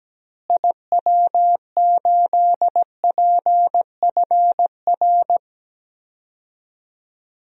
Morse code, heard as a keyed tone, about 17 wpm, 700 Hz. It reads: IW8PFR
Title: DXCC-100 Call Signs - 6 Characters - Repeated 3x - 17wpm